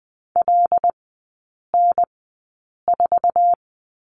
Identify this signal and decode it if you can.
Morse code, heard as a keyed tone, about 20 wpm, spaced as for 8 wpm, 700 Hz. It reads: LN4
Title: 074 Lesson 13e - Letter L - Three Characters 20wpm